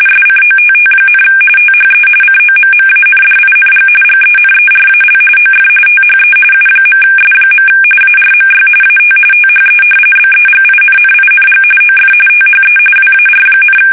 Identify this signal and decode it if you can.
RTTY, about 45.45 baud. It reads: SCOTT, THE COMMANDING GENERAL OF THE U.S. ARMY AT THE BEGINNING OF THE WAR,
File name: fsk850 / usnavy3